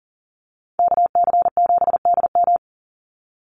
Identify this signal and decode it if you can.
Morse code, heard as a keyed tone, about 40 wpm, 700 Hz. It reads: KC7DM
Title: NA SOTA Activator Callsigns 40wpm